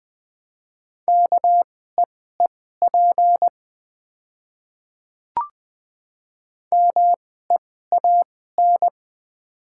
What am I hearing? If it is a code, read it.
Morse code, heard as a keyed tone, about 20 wpm, spaced as for 14 wpm, 700 Hz. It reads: KEEP MEAN
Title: 2x Character Spacing - Top 300 Words - 20wpm